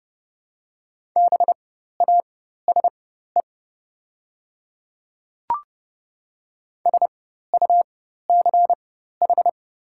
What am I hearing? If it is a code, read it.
Morse code, heard as a keyed tone, about 30 wpm, spaced as for 14 wpm, 700 Hz. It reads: BASE SUCH
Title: 4x Character Spacing - Top 500 Words - 30wpm